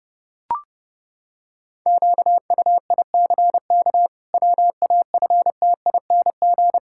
Morse code, heard as a keyed tone, about 30 wpm, 700 Hz. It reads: QUICK WAFTING
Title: Pangram Challenge v2 - 30wpm